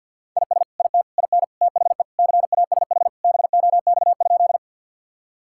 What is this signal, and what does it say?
Morse code, heard as a keyed tone, about 52 wpm, 700 Hz. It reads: IS IT IN THE CASH BOX?
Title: Sentences from Top 1000 Words - Review Part 5 - 50wpm